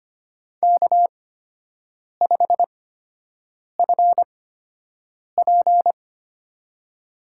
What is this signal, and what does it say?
Morse code, heard as a keyed tone, about 25 wpm, spaced as for 7 wpm, 700 Hz. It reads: K5FP
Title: USA Call Signs - 8x Character Spacing - 25wpm